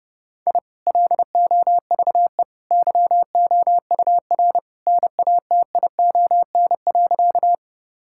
Morse code, heard as a keyed tone, about 32 wpm, 700 Hz. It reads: I LOVE YOUR NATION.
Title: Sentences from Top 1000 Words - Review Part 6 - 30wpm